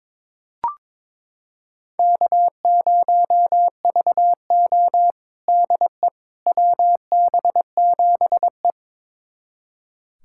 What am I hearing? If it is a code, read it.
Morse code, heard as a keyed tone, about 22 wpm, 700 Hz. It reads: K0VO DE WB7E
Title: USA Callsign DE Callsign - 4 Characters - 22wpm